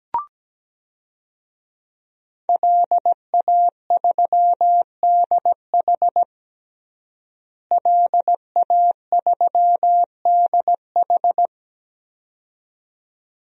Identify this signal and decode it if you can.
Morse code, heard as a keyed tone, about 17 wpm, 700 Hz. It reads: LA3DH LA3DH
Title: DXCC-100 Call Signs - 5 Characters - Repeated 2x - 17wpm